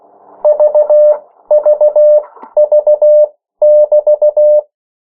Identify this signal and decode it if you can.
Morse code, heard as a keyed tone, about 16 wpm, spaced as for 13 wpm, 600 Hz. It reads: VVV<BT>
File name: morse mp3 / 14 urok 13